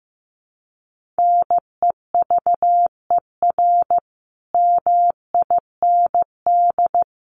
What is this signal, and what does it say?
Morse code, heard as a keyed tone, about 14 wpm, 700 Hz. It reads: NEVER MIND
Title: Sentences from Top 1400 Words 15wpm